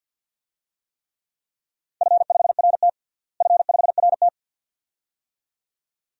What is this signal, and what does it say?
Morse code, heard as a keyed tone, about 52 wpm, 700 Hz. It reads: F5RT F5RT
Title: DXCC-100 Call Signs - 4 Characters - Repeated 2x - 50wpm